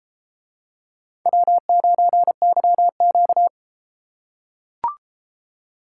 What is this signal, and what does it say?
Morse code, heard as a keyed tone, about 30 wpm, 700 Hz. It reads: W9YQ